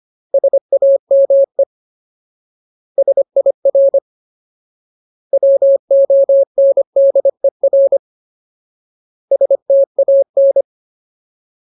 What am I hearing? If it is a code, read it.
Morse code, heard as a keyed tone, about 25 wpm, 550 Hz. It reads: SAME SIR WONDER STAN
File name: Ultimate Word List / 25WPM Ultimate